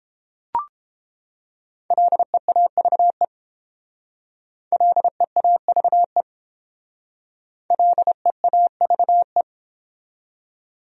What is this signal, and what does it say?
Morse code, heard as a keyed tone, about 30 wpm, 700 Hz. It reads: LEAVE LEAVE LEAVE